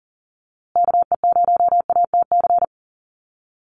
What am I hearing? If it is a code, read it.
Morse code, heard as a keyed tone, about 40 wpm, 700 Hz. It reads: KE0ATC
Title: USA Call Signs Repeated 3x 40wpm